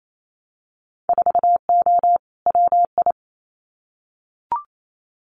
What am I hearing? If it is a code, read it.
Morse code, heard as a keyed tone, about 28 wpm, 700 Hz. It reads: 4O WI
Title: Winter Field Day Exchanges 28wpm